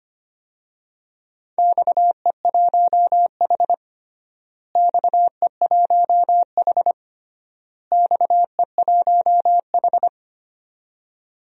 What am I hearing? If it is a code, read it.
Morse code, heard as a keyed tone, about 25 wpm, 700 Hz. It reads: XE1H XE1H XE1H
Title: DXCC-100 Call Signs - 4 Characters - Repeated 3x - 25wpm